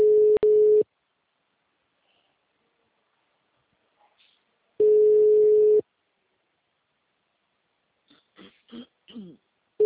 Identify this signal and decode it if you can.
Ringback tone (Europe, 425 Hz).
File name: out- -2036- - - .33422